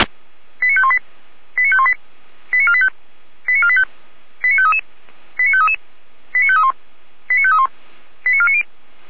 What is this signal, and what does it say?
Five-tone selective call (ZVEI), heard as quick runs of five tones, tones of about 70 ms. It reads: 89518 89518 89485 89485 89533 89533 89531 89531 89490